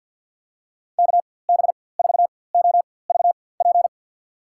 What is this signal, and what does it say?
Morse code, heard as a keyed tone, about 52 wpm, spaced as for 23 wpm, 700 Hz. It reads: KB4OVP
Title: USA Call Signs - 6 Characters - 4x Character Spacing - 50wpm